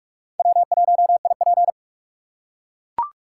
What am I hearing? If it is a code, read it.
Morse code, heard as a keyed tone, about 44 wpm, 700 Hz. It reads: W1IP
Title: USA Call Signs - 4 Characters - 2x Character Spacing - 45wpm